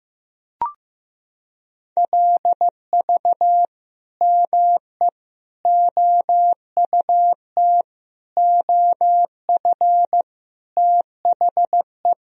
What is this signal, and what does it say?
Morse code, heard as a keyed tone, about 15 wpm, 700 Hz. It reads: LV ME OUT OF THE
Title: Sentences from Top 400 Words and Abbreviations 15wpm